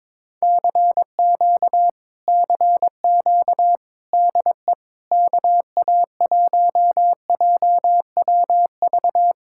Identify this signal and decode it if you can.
Morse code, heard as a keyed tone, about 22 wpm, 700 Hz. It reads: CQ CQ DE KA1JWV
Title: Speed-Racing Common QSO Phrases - 22wpm